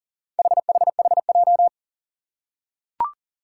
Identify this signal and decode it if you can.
Morse code, heard as a keyed tone, about 40 wpm, 700 Hz. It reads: HHHJ
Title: Four Challenging Letter-Number Combinations 40wpm